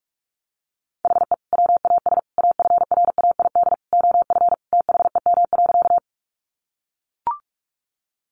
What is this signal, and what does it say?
Morse code, heard as a keyed tone, about 44 wpm, 700 Hz. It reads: HE WAS AFRAID OF THEM.
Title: Sentences from Top 1500 Words 45wpm